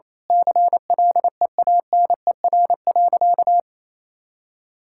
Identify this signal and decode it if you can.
Morse code, heard as a keyed tone, about 28 wpm, 700 Hz. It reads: CLEANER.
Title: Pangram Challenge v2 - 28wpm